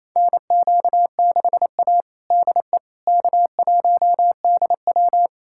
Morse code, heard as a keyed tone, about 27 wpm, 700 Hz. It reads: NQ6A DE K1DW
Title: USA Callsign DE Callsign - 4 Characters - 28wpm